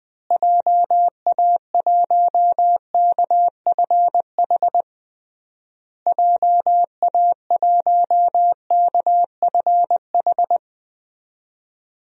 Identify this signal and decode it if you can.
Morse code, heard as a keyed tone, about 21 wpm, 700 Hz. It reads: JA1KFH JA1KFH
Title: DXCC-100 Call Signs - 6 Characters - Repeated 3x - 20wpm